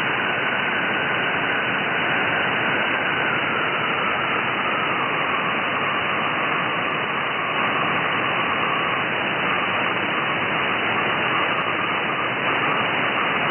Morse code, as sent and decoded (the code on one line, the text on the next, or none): none